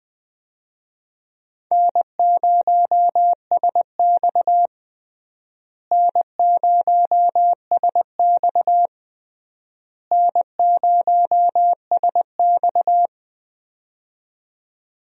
Morse code, heard as a keyed tone, about 21 wpm, 700 Hz. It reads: N0SX N0SX N0SX
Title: USA Call Signs - 4 Characters - Repeated 3x - 20wpm